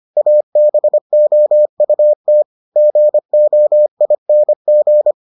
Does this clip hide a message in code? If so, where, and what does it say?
Yes: Morse code from 0.2 to 5.1 s, heard as a keyed tone, about 25 wpm, 600 Hz. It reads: ABOUT GOING